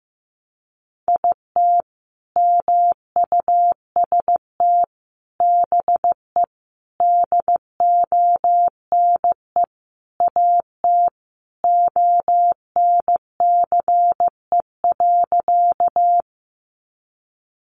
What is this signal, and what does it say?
Morse code, heard as a keyed tone, about 15 wpm, 700 Hz. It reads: IT MUST BE DONE AT ONCE.
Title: Sentences from Top 500 Words 15wpm